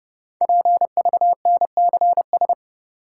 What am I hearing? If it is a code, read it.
Morse code, heard as a keyed tone, about 30 wpm, 700 Hz. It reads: PVNCS
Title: Code Groups - 5 Letter:Numbers - 30wpm